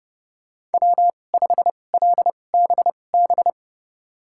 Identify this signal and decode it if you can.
Morse code, heard as a keyed tone, about 30 wpm, spaced as for 22 wpm, 700 Hz. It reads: W5LBB
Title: USA Call Signs - 5 Characters - 2x Character Spacing - 30wpm